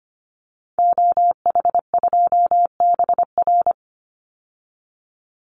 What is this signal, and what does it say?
Morse code, heard as a keyed tone, about 25 wpm, 700 Hz. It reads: OH2BR
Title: Call Signs - 25wpm